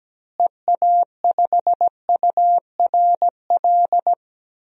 Morse code, heard as a keyed tone, about 17 wpm, 700 Hz. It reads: EA5URL
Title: DXCC-100 Call Signs - 6 Characters - Repeated 2x - 17wpm